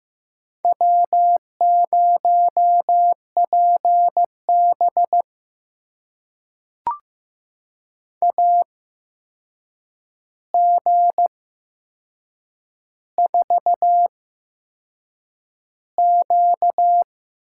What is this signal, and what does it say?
Morse code, heard as a keyed tone, about 15 wpm, spaced as for 6 wpm, 700 Hz. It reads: W0PB AG4Q
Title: USA Call Signs - 8x Character Spacing - 15wpm